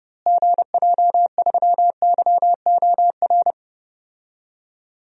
Morse code, heard as a keyed tone, about 32 wpm, 700 Hz. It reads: GJ3YOR
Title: DXCC-100 Call Signs - 6 Characters - Repeated 3x - 30wpm